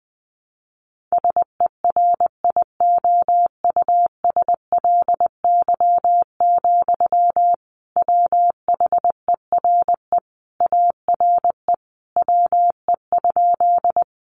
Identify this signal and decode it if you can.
Morse code, heard as a keyed tone, about 20 wpm, 700 Hz. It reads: SERIOUSLY, WHERE ARE WE?
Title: Sentences from Top 1200 Words 20wpm